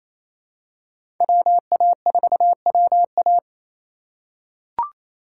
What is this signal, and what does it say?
Morse code, heard as a keyed tone, about 28 wpm, 700 Hz. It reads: WA4WA
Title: USA Call Signs - 5 Characters - 2x Character Spacing - 28wpm